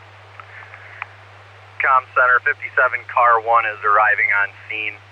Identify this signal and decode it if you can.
Emergency Alert System (SAME) tones, heard as harsh buzzing bursts, with speech. It is heard from 0.5 to 4.6 s.